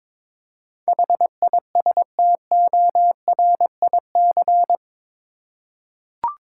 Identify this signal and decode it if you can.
Morse code, heard as a keyed tone, about 22 wpm, 700 Hz. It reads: HISTORIC
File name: practice / 22wpm / Eight Letter Words - 22wpm